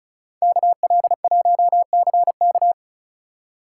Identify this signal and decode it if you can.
Morse code, heard as a keyed tone, about 44 wpm, 700 Hz. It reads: KL1CK
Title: Speed Racing - USA 2x2 Call Signs 35wpm